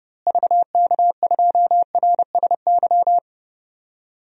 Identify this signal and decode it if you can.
Morse code, heard as a keyed tone, about 30 wpm, 700 Hz. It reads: VK2RSY
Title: DXCC-100 Call Signs - 6 Characters - Repeated 2x - 30wpm